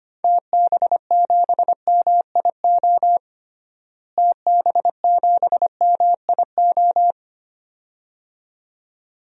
Morse code, heard as a keyed tone, about 25 wpm, 700 Hz. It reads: TB7MIO TB7MIO
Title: DXCC-100 Call Signs - 6 Characters - Repeated 2x - 25wpm